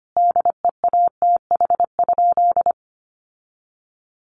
Morse code, heard as a keyed tone, about 25 wpm, 700 Hz. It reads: DEATH?
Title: Sentences from Top 1300 Words 25wpm